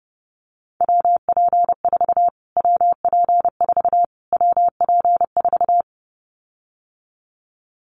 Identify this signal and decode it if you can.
Morse code, heard as a keyed tone, about 30 wpm, 700 Hz. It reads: WP4 WP4 WP4
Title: Callsign Prefixes for Top 25 DXCC - Repeated 3x - 30wpm